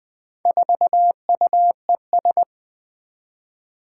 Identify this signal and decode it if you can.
Morse code, heard as a keyed tone, about 20 wpm, 700 Hz. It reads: 4UES
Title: Four Challenging Letter-Number Combinations 20wpm